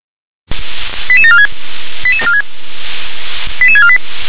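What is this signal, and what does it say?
Five-tone selective call (ZVEI), heard as quick runs of five tones, tones of about 70 ms. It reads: 88746 88648
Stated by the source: ZVEI five-tone call received on SDR.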